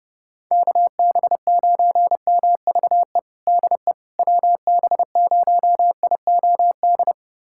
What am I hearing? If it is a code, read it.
Morse code, heard as a keyed tone, about 32 wpm, 700 Hz. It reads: KB9MVE DE WB0IOD
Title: USA Callsign DE Callsign 30wpm